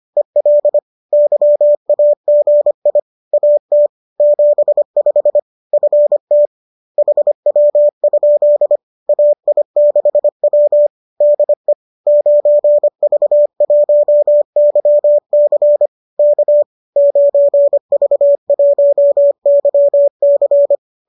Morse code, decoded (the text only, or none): EL YAGI AT 75 FT HW? AI6W DE 9V1YC K 9V1YC